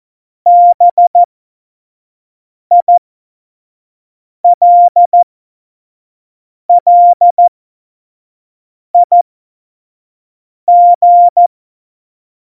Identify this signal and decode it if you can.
Morse code, heard as a keyed tone, about 14 wpm, spaced as for 5 wpm, 700 Hz. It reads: BILLIG
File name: Morsekurs / Witze / Witz 15. Januar 2019, 25 F70